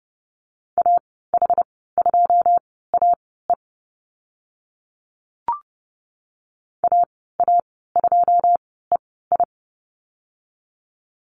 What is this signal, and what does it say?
Morse code, heard as a keyed tone, about 30 wpm, spaced as for 17 wpm, 700 Hz. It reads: AH2AE AA2EI
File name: Lesson 14g - Number 2 - Call Signs 30wpm